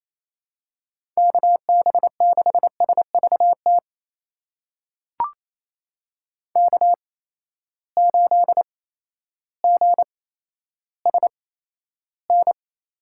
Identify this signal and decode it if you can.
Morse code, heard as a keyed tone, about 28 wpm, spaced as for 13 wpm, 700 Hz. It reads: KB6SVT K8GSN
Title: USA Call Signs - 8x Character Spacing - 28wpm